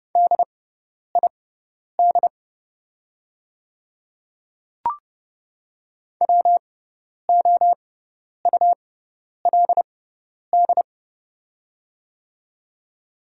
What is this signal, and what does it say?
Morse code, heard as a keyed tone, about 30 wpm, spaced as for 10 wpm, 700 Hz. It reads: DID WOULD
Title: 6x Character Spacing - Top 100 Words - 30wpm